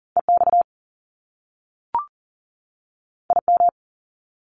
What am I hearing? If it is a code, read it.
Morse code, heard as a keyed tone, about 40 wpm, 700 Hz. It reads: EX IM
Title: Top Two Letter Combinations in Words 40wpm